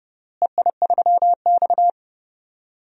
Morse code, heard as a keyed tone, about 30 wpm, 700 Hz. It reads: EI3X
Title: DXCC-100 Call Signs - 4 Characters - Repeated 3x - 30wpm